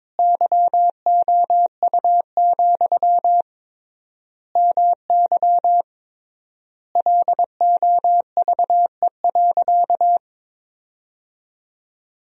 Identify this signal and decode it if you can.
Morse code, heard as a keyed tone, about 22 wpm, 700 Hz. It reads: YOU, MY LOVE.
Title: 3x Word-Spacing - Sentences from Top 400 Words - 22wpm